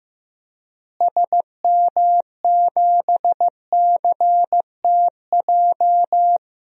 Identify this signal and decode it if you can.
Morse code, heard as a keyed tone, about 15 wpm, 700 Hz. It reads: SM7CTJ
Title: DXCC-100 Call Signs - 6 Characters - Repeated 2x - 15wpm